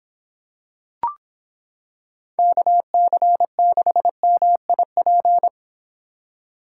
Morse code, heard as a keyed tone, about 26 wpm, 700 Hz. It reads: KC6MIP